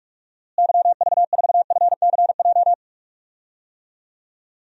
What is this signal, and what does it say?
Morse code, heard as a keyed tone, about 43 wpm, 700 Hz. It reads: YV4FCJ
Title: DXCC-100 Call Signs - 6 Characters - Repeated 3x - 45wpm